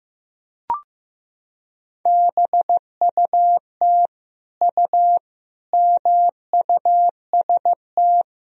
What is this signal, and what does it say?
Morse code, heard as a keyed tone, about 15 wpm, 700 Hz. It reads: BUT U MUST
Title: Sentences from Top 300 Words and Abbreviations 15wpm